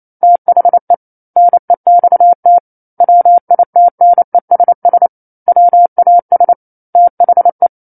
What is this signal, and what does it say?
Morse code, heard as a keyed tone, about 28 wpm, 700 Hz. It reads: THE NEXT WITNESS WAS THE
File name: 28 WPM